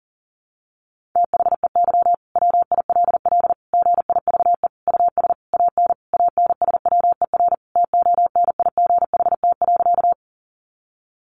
Morse code, heard as a keyed tone, about 40 wpm, 700 Hz. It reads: THEY WILL GIVE US AN ANSWER TONIGHT.
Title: Sentences from Top 600 Words 40wpm